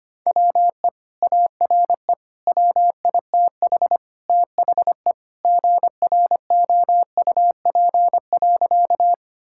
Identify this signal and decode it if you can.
Morse code, heard as a keyed tone, about 25 wpm, 700 Hz. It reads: WE ARE WITH THE GROUP.
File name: practice / 25wpm / Sentences with 2-Word Phrases - 25wpm